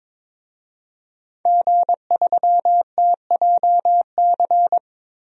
Morse code, heard as a keyed tone, about 22 wpm, 700 Hz. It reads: G3TJC
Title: DXCC-100 Call Signs - 5 Characters - 22wpm